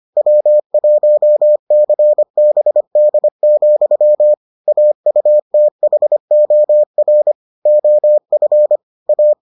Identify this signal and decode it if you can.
Morse code, heard as a keyed tone, about 25 wpm, 600 Hz. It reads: W1CBD, AUTHOR OF A